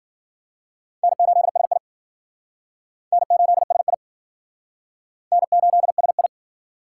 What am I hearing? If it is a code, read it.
Morse code, heard as a keyed tone, about 51 wpm, 700 Hz. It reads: N8SI N8SI N8SI